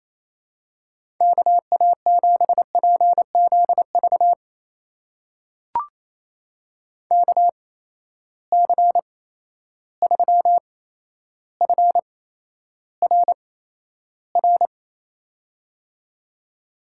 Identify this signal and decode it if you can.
Morse code, heard as a keyed tone, about 27 wpm, spaced as for 12 wpm, 700 Hz. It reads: KA7PZV KC3FRR
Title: USA Call Signs - 6 Characters - 8x Character Spacing - 28wpm